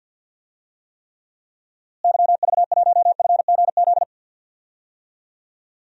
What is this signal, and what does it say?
Morse code, heard as a keyed tone, about 49 wpm, 700 Hz. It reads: YV1FGZ
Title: DXCC-100 Call Signs - 6 Characters - Repeated 3x - 50wpm